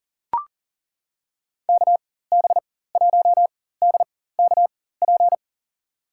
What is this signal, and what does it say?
Morse code, heard as a keyed tone, about 40 wpm, spaced as for 19 wpm, 700 Hz. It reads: KB1DKP
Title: USA Call Signs - 6 Characters - 4x Character Spacing - 40wpm